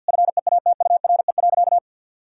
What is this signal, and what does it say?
Morse code, heard as a keyed tone, about 50 wpm, 700 Hz. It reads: FEATURE.
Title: Sentences from Top 1000 Words - Review Part 5 - 50wpm